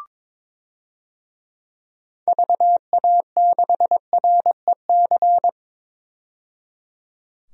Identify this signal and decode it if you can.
Morse code, heard as a keyed tone, about 22 wpm, 700 Hz. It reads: VA6REC
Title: DXCC-100 Call Signs - 6 Characters - 22wpm